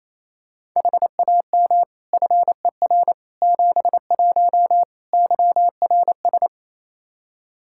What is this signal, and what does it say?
Morse code, heard as a keyed tone, about 28 wpm, 700 Hz. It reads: HAM FER 71 YRS
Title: Practice QSO - Exchange 3 - More Info - 28wpm